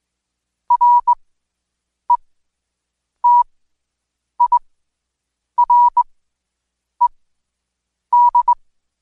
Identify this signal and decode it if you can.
Morse code, heard as a keyed tone, about 21 wpm, spaced as for 7 wpm, 970 Hz. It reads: RETIRED
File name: Code Quick / AudioDownld2 / Seg113-Day18-DayDrill 18